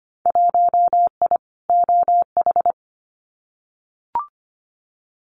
Morse code, heard as a keyed tone, about 25 wpm, 700 Hz. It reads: 1I OH
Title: Winter Field Day Exchanges 25wpm